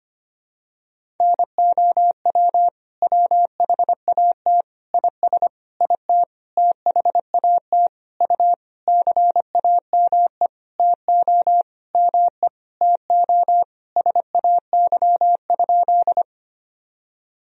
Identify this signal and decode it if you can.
Morse code, heard as a keyed tone, about 25 wpm, 700 Hz. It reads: NOW WHAT IS IT THAT U CAME TO ME TO SAY?
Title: Sentences from Top 200 Words and Abbreviations 25wpm